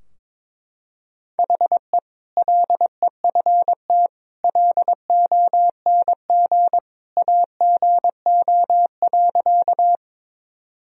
Morse code, heard as a keyed tone, about 22 wpm, 700 Hz. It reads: HE LEFT LONG AGO.
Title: Sentences from Top 1000 Words - Review Part 3 - 22wpm